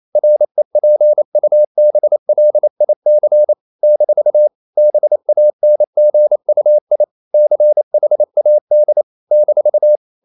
Morse code, heard as a keyed tone, about 28 wpm, 600 Hz. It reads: REPUBLIC - BANGUI CHAD -
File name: CW / Practice / NationalCapitals28